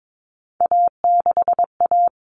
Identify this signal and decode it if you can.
Morse code, heard as a keyed tone, about 22 wpm, 700 Hz. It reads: A6A
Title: Common Cut Numbers - 3-Digit 22wpm